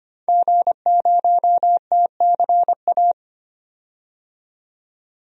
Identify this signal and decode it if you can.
Morse code, heard as a keyed tone, about 25 wpm, 700 Hz. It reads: G0TCA
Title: DXCC-100 Call Signs - 5 Characters - 25wpm